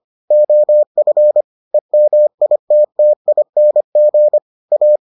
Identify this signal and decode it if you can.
Morse code, heard as a keyed tone, about 25 wpm, 600 Hz. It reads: OF EMITTING A